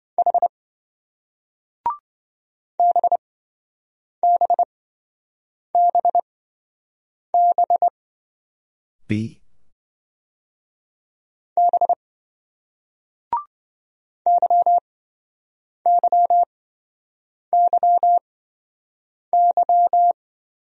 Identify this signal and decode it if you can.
Morse code, heard as a keyed tone, about 27 wpm, spaced as for 7 wpm, 700 Hz. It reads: H BBBB B YYYY